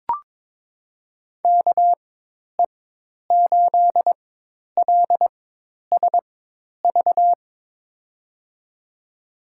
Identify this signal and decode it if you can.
Morse code, heard as a keyed tone, about 22 wpm, spaced as for 10 wpm, 700 Hz. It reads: KE8LSV
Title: USA Call Signs - 4x Character Spacing - 22wpm